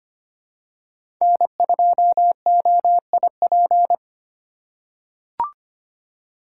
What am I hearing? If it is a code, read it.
Morse code, heard as a keyed tone, about 25 wpm, 700 Hz. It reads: N2OIP
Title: USA Call Signs - 5 Characters - 3x Character Spacing - 25wpm